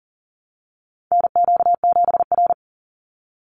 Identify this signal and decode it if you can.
Morse code, heard as a keyed tone, about 40 wpm, 700 Hz. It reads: NQ7R
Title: NA SOTA Chaser Callsigns 40wpm